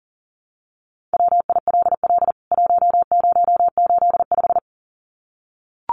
Morse code, heard as a keyed tone, about 40 wpm, 700 Hz. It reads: WILL 1085
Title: CWT Exchange 40wpm